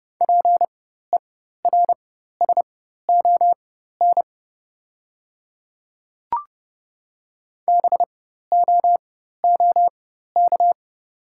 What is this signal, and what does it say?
Morse code, heard as a keyed tone, about 30 wpm, spaced as for 14 wpm, 690 Hz. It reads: PERSON BOOK
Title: 4x Character Spacing - Top 500 Words - 30wpm